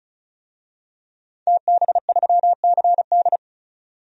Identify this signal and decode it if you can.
Morse code, heard as a keyed tone, about 34 wpm, 700 Hz. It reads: TB3CD
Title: DXCC-100 Call Signs - 5 Characters - 35wpm